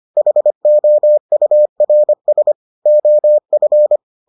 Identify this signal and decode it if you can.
Morse code, heard as a keyed tone, about 25 wpm, 600 Hz. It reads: HOURS OF